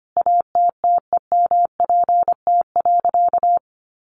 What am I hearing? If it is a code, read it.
Morse code, heard as a keyed tone, about 25 wpm, 700 Hz. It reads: ATTEMPT.
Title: Sentences from Top 1000 Words 25wpm